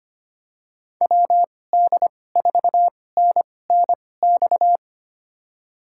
Morse code, heard as a keyed tone, about 25 wpm, spaced as for 18 wpm, 700 Hz. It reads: WD4NNX
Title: USA Call Signs - 6 Characters - 2x Character Spacing - 25wpm